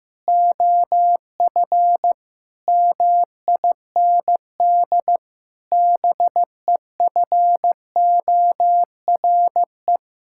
Morse code, heard as a keyed tone, about 15 wpm, 700 Hz. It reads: OF MIND BEFORE
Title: Sentences from Top 1000 Words - Review Part 6 - 15wpm